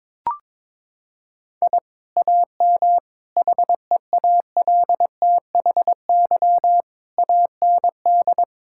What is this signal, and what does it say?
Morse code, heard as a keyed tone, about 22 wpm, 700 Hz. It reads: I AM HEALTHY AND
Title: Sentences from Top 1000 Words - Review Part 5 - 22wpm